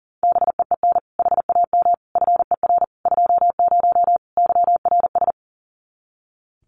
Morse code, heard as a keyed tone, about 40 wpm, 700 Hz. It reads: BEEN HAM FER 20 YRS
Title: Common QSO Phrases 40wpm